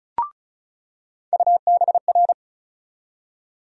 Morse code, heard as a keyed tone, about 36 wpm, 700 Hz. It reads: UBR
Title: USA Call Sign Postfixes - 3 Letters 35wpm